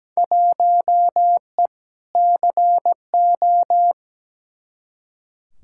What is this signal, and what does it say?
Morse code, heard as a keyed tone, about 17 wpm, 700 Hz. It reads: 1E CO
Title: Field Day Exchanges - 17wpm